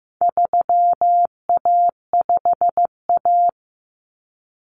Morse code, heard as a keyed tone, about 15 wpm, 700 Hz. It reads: 3A5A
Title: Common Cut Numbers - 4-Digit 15wpm